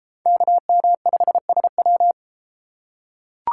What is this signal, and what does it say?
Morse code, heard as a keyed tone, about 33 wpm, 700 Hz. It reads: KM5SW